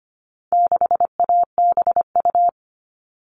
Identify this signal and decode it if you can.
Morse code, heard as a keyed tone, about 25 wpm, 700 Hz. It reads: 6ABU
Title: All Cut Numbers - 4-Digit 25wpm